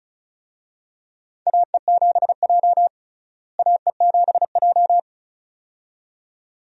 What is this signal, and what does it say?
Morse code, heard as a keyed tone, about 34 wpm, 700 Hz. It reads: AE7J AE7J
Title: USA Call Signs - 4 Characters - Repeated 2x - 35wpm